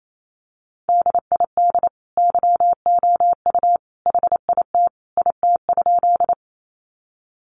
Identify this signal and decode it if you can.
Morse code, heard as a keyed tone, about 28 wpm, 700 Hz. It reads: DID YOU HIT IT?
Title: Sentences from Top 1200 Words 28wpm